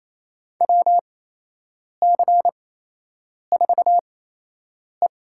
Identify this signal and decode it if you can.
Morse code, heard as a keyed tone, about 28 wpm, spaced as for 8 wpm, 700 Hz. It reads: WC4E
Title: USA Call Signs - 4 Characters - 8x Character Spacing - 28wpm